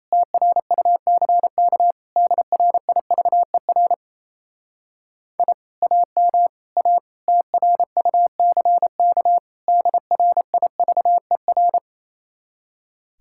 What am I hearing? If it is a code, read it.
Morse code, heard as a keyed tone, about 38 wpm, 700 Hz. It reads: TRUCK DRIVER I AM A TRUCK DRIVER